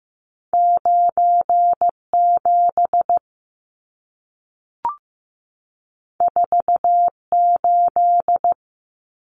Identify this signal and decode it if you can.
Morse code, heard as a keyed tone, about 15 wpm, 700 Hz. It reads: 97 48
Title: Numbers - 2-Digit 15wpm